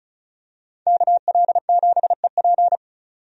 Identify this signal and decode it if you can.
Morse code, heard as a keyed tone, about 34 wpm, 700 Hz. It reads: KL7EP
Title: DXCC-100 Call Signs - 5 Characters - 35wpm